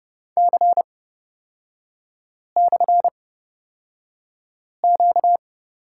Morse code, heard as a keyed tone, about 30 wpm, spaced as for 5 wpm, 700 Hz. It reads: C/Q
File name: Lesson 38e - Letter q - Three Characters 30wpm